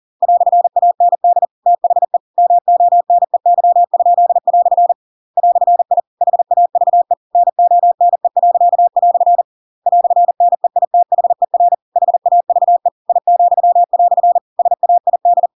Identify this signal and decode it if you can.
Morse code, heard as a keyed tone, about 40 wpm, 700 Hz. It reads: "AND THE MONEY?" "I HAVE NONE." "NEITHER HAVE I," SAID
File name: Pinocchio / Pinocchio ch8 40 40 ik1hge